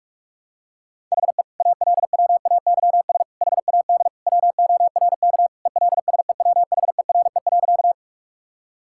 Morse code, heard as a keyed tone, about 43 wpm, 700 Hz. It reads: HE ALWAYS HAD WORK ELSEWHERE.
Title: Sentences from Top 1600 Words 45wpm